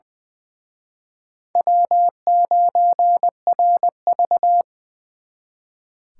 Morse code, heard as a keyed tone, about 24 wpm, 700 Hz. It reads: W9RV
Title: Speed Racing - USA 1x2 Call Signs 20wpm